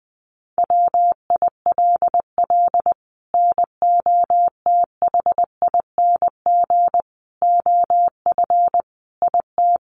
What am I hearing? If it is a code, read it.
Morse code, heard as a keyed tone, about 20 wpm, 700 Hz. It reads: WILL NOTHING OF IT
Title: Sentences from Top 1200 Words 20wpm